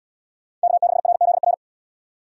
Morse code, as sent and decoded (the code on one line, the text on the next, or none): -.- -.... .-. --.. .-.
K6RZR